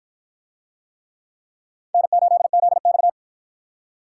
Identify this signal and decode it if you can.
Morse code, heard as a keyed tone, about 53 wpm, 700 Hz. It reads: N8ZX